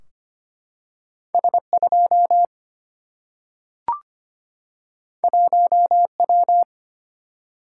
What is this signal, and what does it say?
Morse code, heard as a keyed tone, about 25 wpm, 700 Hz. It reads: S2 1W
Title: Single-Letter Single-Number 25wpm